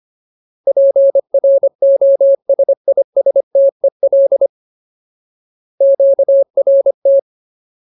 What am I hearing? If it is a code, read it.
Morse code, heard as a keyed tone, about 25 wpm, 550 Hz. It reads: PROSISTEL QRT
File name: Ultimate Word List / 25WPM Ultimate